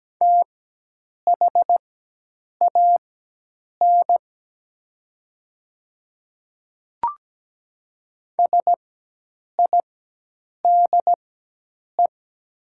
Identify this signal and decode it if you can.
Morse code, heard as a keyed tone, about 17 wpm, spaced as for 8 wpm, 700 Hz. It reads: THAN SIDE
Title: 4x Character Spacing - Top 200 Words - 17wpm